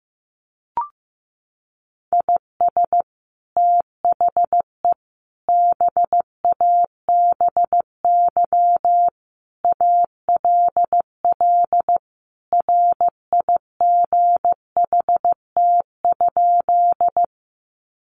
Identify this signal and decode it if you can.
Morse code, heard as a keyed tone, about 15 wpm, 700 Hz. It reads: IS THE BABY ALL RIGHT?
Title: Sentences from Top 1300 Words 15wpm